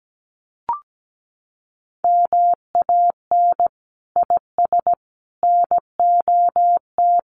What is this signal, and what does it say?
Morse code, heard as a keyed tone, about 17 wpm, 700 Hz. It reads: MAN IS NOT
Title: Sentences from Top 1400 Words 17wpm